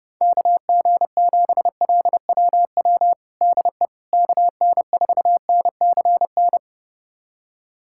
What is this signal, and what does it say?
Morse code, heard as a keyed tone, about 30 wpm, 700 Hz. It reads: KG7LWW DE KN4NCN
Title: USA Callsign DE Callsign 30wpm